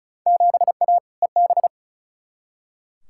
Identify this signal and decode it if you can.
Morse code, heard as a keyed tone, about 36 wpm, 700 Hz. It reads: 7A EB
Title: Field Day Exchanges Encore - 35wpm